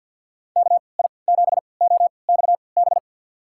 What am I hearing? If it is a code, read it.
Morse code, heard as a keyed tone, about 53 wpm, spaced as for 29 wpm, 700 Hz. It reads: KI7OXB
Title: USA Call Signs - 6 Characters - 3x Character Spacing - 50wpm